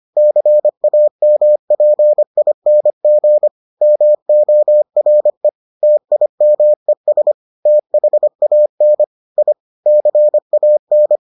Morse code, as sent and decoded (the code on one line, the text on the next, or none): -.-. .- -- .--. .. -. --. / -- --- .-. / - .. -- . ... / - .... .- -. / .. / -.-. .- -.
CAMPING MORE TIMES THAN I CAN